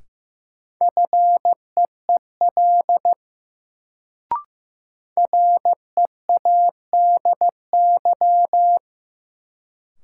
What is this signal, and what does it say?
Morse code, heard as a keyed tone, about 16 wpm, 700 Hz. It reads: FEEL READY
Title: Sentences from Top 400 Words - New Word Drill - 15wpm